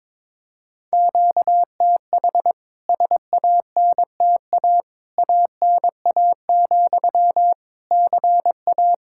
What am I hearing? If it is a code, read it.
Morse code, heard as a keyed tone, about 21 wpm, 700 Hz. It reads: QTH SANTA ANA, CA
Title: K1USN SST QSO Practice 22wpm